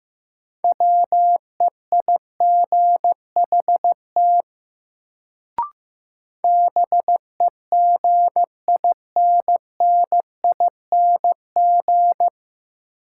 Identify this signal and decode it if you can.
Morse code, heard as a keyed tone, about 15 wpm, 700 Hz. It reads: WEIGHT BEGINNING
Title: Sentences from Top 800 Words - New Word Drill - 15wpm